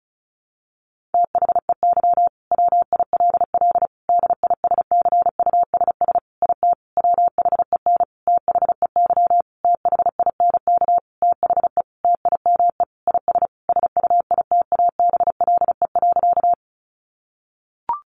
Morse code, heard as a keyed tone, about 34 wpm, 700 Hz. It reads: THEY WILL DISCUSS IT WHEN THEY THINK THE TIME IS SUITABLE.
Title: Sentences from Top 1400 Words 35wpm